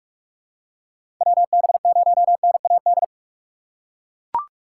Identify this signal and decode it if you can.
Morse code, heard as a keyed tone, about 41 wpm, 690 Hz. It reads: WB0NAD
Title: USA Call Signs - 4x Character Spacing - 45wpm